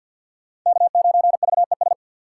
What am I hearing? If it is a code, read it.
Morse code, heard as a keyed tone, about 46 wpm, 700 Hz. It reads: K9VES
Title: USA Call Signs - 5 Characters - 6x Character Spacing - 50wpm